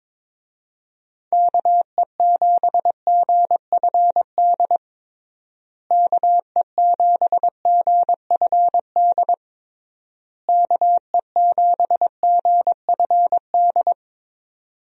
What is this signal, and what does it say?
Morse code, heard as a keyed tone, about 22 wpm, 700 Hz. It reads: KE7GFD KE7GFD KE7GFD
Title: USA Call Signs - 6 Characters - Repeated 3x - 22wpm